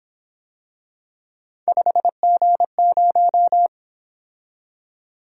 Morse code, heard as a keyed tone, about 27 wpm, 700 Hz. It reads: 5G0
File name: Speed-Racing - Three Letter-Number Combinations - 22wpm